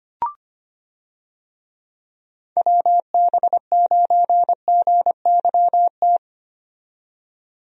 Morse code, heard as a keyed tone, about 25 wpm, 700 Hz. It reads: WB9GYT
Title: USA Call Signs - 6 Characters - 25wpm